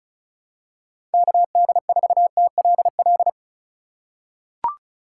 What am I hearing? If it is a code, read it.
Morse code, heard as a keyed tone, about 36 wpm, 700 Hz. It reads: KD4TLL
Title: USA Call Signs - 4x Character Spacing - 35wpm